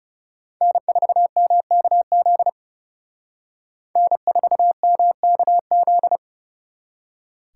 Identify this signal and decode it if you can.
Morse code, heard as a keyed tone, about 38 wpm, 700 Hz. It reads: N4MKZ N4MKZ